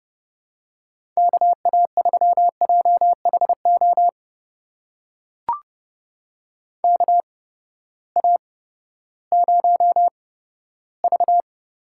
Morse code, heard as a keyed tone, about 32 wpm, spaced as for 15 wpm, 700 Hz. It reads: KA3JHO KA0V
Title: USA Call Signs - 6 Characters - 8x Character Spacing - 30wpm